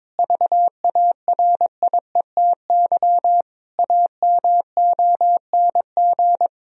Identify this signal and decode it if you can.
Morse code, heard as a keyed tone, about 22 wpm, 700 Hz. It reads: VARIETY AMONG
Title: Sentences from Top 1000 Words - Review Part 4 - 22wpm